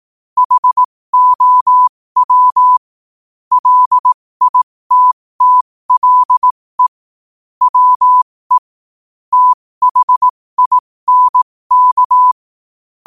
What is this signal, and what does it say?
Morse code, heard as a keyed tone, about 18 wpm, spaced as for 15 wpm, 1000 Hz. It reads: HOW LITTLE WE THINK